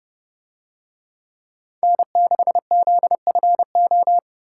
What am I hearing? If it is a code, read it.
Morse code, heard as a keyed tone, about 31 wpm, 700 Hz. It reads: N6ZFO